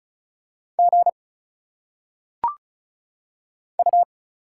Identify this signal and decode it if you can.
Morse code, heard as a keyed tone, about 35 wpm, 700 Hz. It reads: G U
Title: Single Letters - Phonetic Pronunciation 35wpm